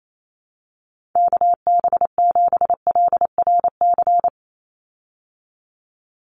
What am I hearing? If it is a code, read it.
Morse code, heard as a keyed tone, about 28 wpm, 700 Hz. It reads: KB7LRC
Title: USA Call Signs 28wpm